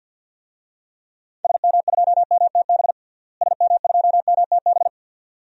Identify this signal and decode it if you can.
Morse code, heard as a keyed tone, about 52 wpm, 700 Hz. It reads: SM2MTB SM2MTB
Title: DXCC-100 Call Signs - 6 Characters - Repeated 2x - 50wpm